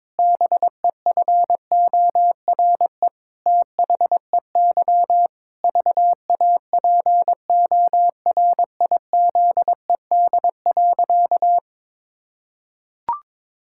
Morse code, heard as a keyed tone, about 22 wpm, 700 Hz. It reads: BEFORE THEY VAPORIZED.
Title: Pangram Challenge v2 - 22wpm